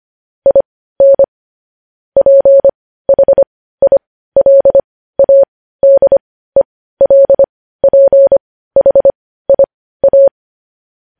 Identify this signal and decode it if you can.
Morse code, heard as a keyed tone, about 25 wpm, spaced as for 15 wpm, 550 Hz. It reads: IN PHILADELPHIA
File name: podcast / News headlines-1x-25x15